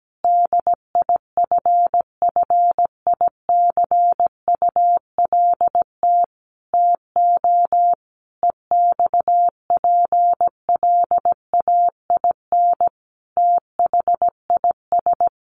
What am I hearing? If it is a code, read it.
Morse code, heard as a keyed tone, about 17 wpm, 700 Hz. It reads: DIFFICULT TO EXPLAIN THIS
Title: Sentences from Top 1100 Words 17wpm